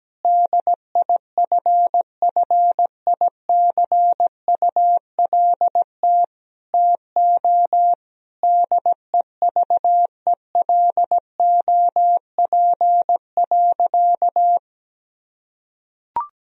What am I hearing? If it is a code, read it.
Morse code, heard as a keyed tone, about 17 wpm, 700 Hz. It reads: DIFFICULT TO DEVELOP.
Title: Sentences from Top 1000 Words - Review Part 2 - 17wpm